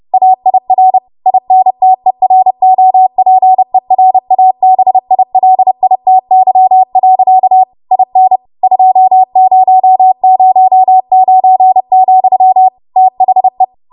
Morse code, heard as a keyed tone, about 30 wpm, 750 Hz. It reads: AIR INTEROPERABILITY. IN 2009, THE
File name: Archive / 30 WPM / 30WPM